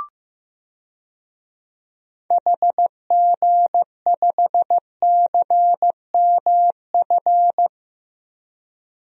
Morse code, heard as a keyed tone, about 15 wpm, 700 Hz. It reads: HG5CMF
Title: DXCC-100 Call Signs - 6 Characters - 15wpm